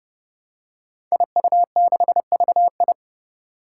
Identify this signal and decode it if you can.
Morse code, heard as a keyed tone, about 30 wpm, 700 Hz. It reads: IU6VI
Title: Five Challenging Letter-Number Combinations 30wpm